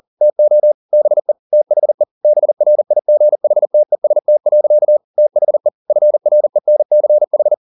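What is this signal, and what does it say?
Morse code, heard as a keyed tone, about 40 wpm, 600 Hz. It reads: TO BE THE BRIGHTEST. THE FRENCH
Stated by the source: Morse code at 40 wpm, practice text, CW med Gnister.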